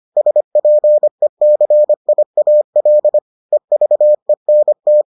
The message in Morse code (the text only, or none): SPECIAL EVENT